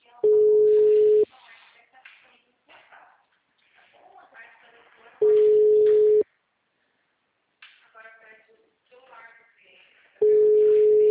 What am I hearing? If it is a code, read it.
Ringback tone (Europe, 425 Hz), with speech.
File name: out- -2002- - - .10011